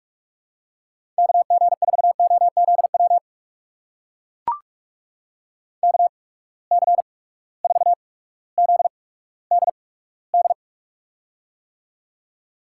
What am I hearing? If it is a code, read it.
Morse code, heard as a keyed tone, about 45 wpm, spaced as for 19 wpm, 700 Hz. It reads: KG4OZW KC4ZDD
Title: USA Call Signs - 6 Characters - 8x Character Spacing - 45wpm